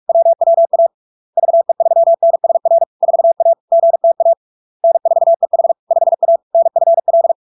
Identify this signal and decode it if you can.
Morse code, heard as a keyed tone, about 46 wpm, 670 Hz. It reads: WWA VE3NSR 4A GTA N4EH 5A NFL
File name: LARK-FD-2022-Log-in-CW-45wpm-670-Hz